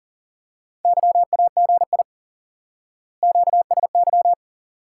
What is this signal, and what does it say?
Morse code, heard as a keyed tone, about 40 wpm, 700 Hz. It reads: YAGI QSY
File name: IQE Course 3 - Lesson 89 - YAGI - Two Elements - 40wpm